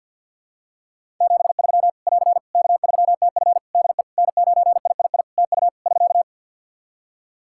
Morse code, heard as a keyed tone, about 47 wpm, 700 Hz. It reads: 73 <AR> K3TF DE N9III TU <SK>
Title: Practice QSO - Exchange 4 - Ending - 50wpm